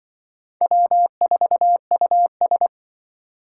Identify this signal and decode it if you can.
Morse code, heard as a keyed tone, about 24 wpm, 700 Hz. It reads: W4US
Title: Speed Racing - USA 1x2 Call Signs 20wpm